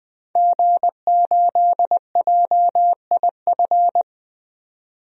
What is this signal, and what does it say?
Morse code, heard as a keyed tone, about 20 wpm, 700 Hz. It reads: G8JIF
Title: DXCC-100 Call Signs - 5 Characters - 20wpm